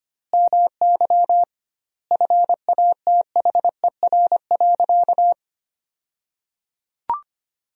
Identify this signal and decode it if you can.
Morse code, heard as a keyed tone, about 25 wpm, 700 Hz. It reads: MY FATHER.
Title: 2x Word-Spacing - Sentences from Top 300 Words - 25wpm